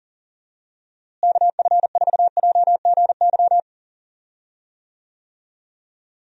Morse code, heard as a keyed tone, about 39 wpm, 700 Hz. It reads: KF4JGY
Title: USA Call Signs - 6 Characters - Repeated 2x - 40wpm